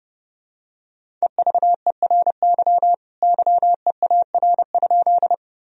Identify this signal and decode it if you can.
Morse code, heard as a keyed tone, about 32 wpm, 700 Hz. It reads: EVERY YEAR?
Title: Sentences from Top 1000 Words - Review Part 1 - 30wpm